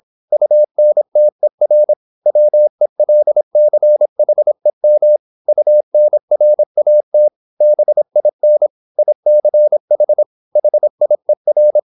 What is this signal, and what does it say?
Morse code, heard as a keyed tone, about 26 wpm, 600 Hz. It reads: UNTER WELCHEM UNRAT BIN ICH HIER